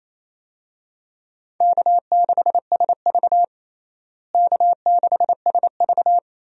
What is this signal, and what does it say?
Morse code, heard as a keyed tone, about 28 wpm, 700 Hz. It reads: K6SV K6SV
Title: USA Call Signs - 4 Characters - Repeated 2x - 28wpm